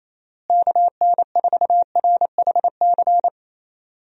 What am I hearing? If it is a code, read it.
Morse code, heard as a keyed tone, about 27 wpm, 700 Hz. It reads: KN4RHC
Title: USA Call Signs - 6 Characters - Repeated 3x - 28wpm